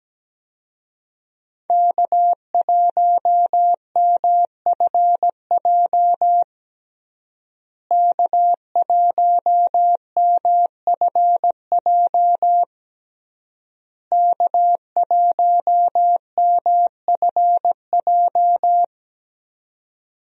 Morse code, heard as a keyed tone, about 17 wpm, 700 Hz. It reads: K1MFJ K1MFJ K1MFJ
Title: USA Call Signs - 5 Characters - Repeated 3x - 17wpm